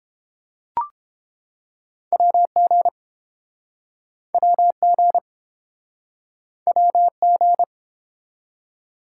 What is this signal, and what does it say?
Morse code, heard as a keyed tone, about 31 wpm, 700 Hz. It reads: WG WG WG